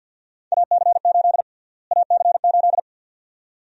Morse code, heard as a keyed tone, about 52 wpm, 700 Hz. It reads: AC8 AC8
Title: USA Call Sign Prefixes - Repeated 2x - 50wpm